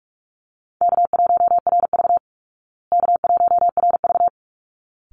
Morse code, heard as a keyed tone, about 44 wpm, 700 Hz. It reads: K1RV K1RV
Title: Call Signs Repeated 3x - 45wpm